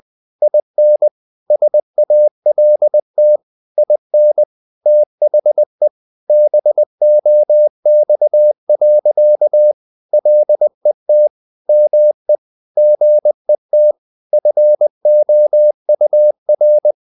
Morse code, heard as a keyed tone, about 20 wpm, 600 Hz. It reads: IN SALT IN THE BOX. LET ME GET FOUR
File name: the old man and the sea / the old man and the sea part 1 20wpm